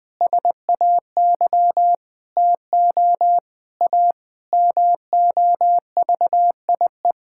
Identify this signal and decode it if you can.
Morse code, heard as a keyed tone, about 20 wpm, 700 Hz. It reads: SAY TO A MOVIE
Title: Sentences with 5-Word Phrases - Spoken Beginning - 20wpm